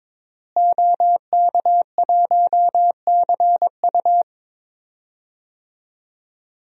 Morse code, heard as a keyed tone, about 23 wpm, 700 Hz. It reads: OK1CU
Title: DXCC-100 Call Signs - 5 Characters - Repeated 2x - 22wpm